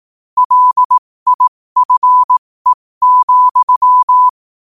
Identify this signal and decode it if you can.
Morse code, heard as a keyed tone, about 18 wpm, spaced as for 15 wpm, 1000 Hz. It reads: LIFE,
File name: qotdfiles / 15wpm